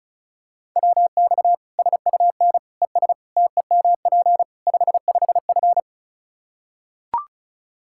Morse code, heard as a keyed tone, about 34 wpm, 700 Hz. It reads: WX SUN ES TEMP 55F
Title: K1USN SST QSO Practice 35wpm